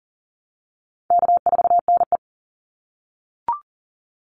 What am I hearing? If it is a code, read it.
Morse code, heard as a keyed tone, about 40 wpm, 700 Hz. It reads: K4NE
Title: Easy Call Signs - 40wpm